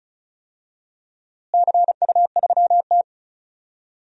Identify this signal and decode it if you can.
Morse code, heard as a keyed tone, about 36 wpm, 700 Hz. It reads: CU3T
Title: DXCC-100 Call Signs - 4 Characters - Repeated 3x - 35wpm